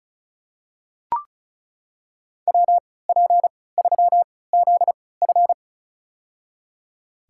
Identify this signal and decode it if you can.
Morse code, heard as a keyed tone, about 34 wpm, spaced as for 20 wpm, 700 Hz. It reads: WP3ZF
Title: USA Call Signs - 5 Characters - 3x Character Spacing - 35wpm